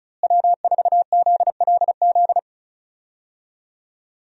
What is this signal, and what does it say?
Morse code, heard as a keyed tone, about 36 wpm, 700 Hz. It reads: W4ZLZ
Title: USA Call Signs - 5 Characters - Repeated 3x - 35wpm